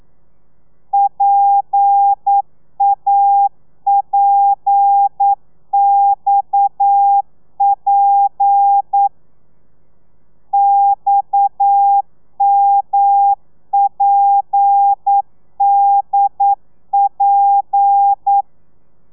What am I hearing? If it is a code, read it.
Morse code, heard as a keyed tone, about 9 wpm, 800 Hz. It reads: PAPXP XMPDP
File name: telegrafia / i2tfj / Lez7 40